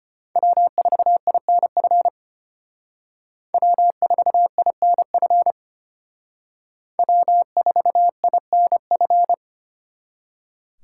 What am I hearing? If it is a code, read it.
Morse code, heard as a keyed tone, about 31 wpm, 700 Hz. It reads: W4INF W4INF W4INF
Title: Speed Racing - USA 1x3 Call Signs 25wpm